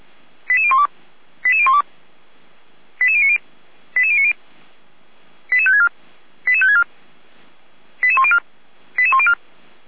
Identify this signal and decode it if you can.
Five-tone selective call (ZVEI), heard as quick runs of five tones, tones of about 70 ms. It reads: 88012 88012 88090 88090 88564 88564 88104 88104